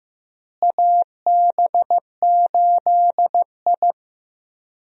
Morse code, heard as a keyed tone, about 15 wpm, 700 Hz. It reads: AB8I
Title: USA Call Signs - 4 Characters - Repeated 2x - 15wpm